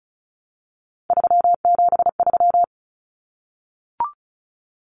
Morse code, heard as a keyed tone, about 34 wpm, 700 Hz. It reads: 373
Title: Numbers - 3-Digit 35wpm